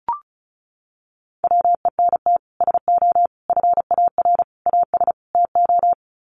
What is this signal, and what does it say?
Morse code, heard as a keyed tone, about 34 wpm, 700 Hz. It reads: WENT SO FAR AS TO
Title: Sets of 5 Words Encore 35wpm